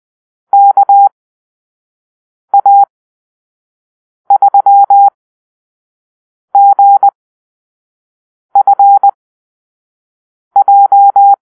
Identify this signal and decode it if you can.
Morse code, heard as a keyed tone, about 20 wpm, spaced as for 5 wpm, 800 Hz. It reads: KA3GFJ
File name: arrl / Call-Sign Drill-4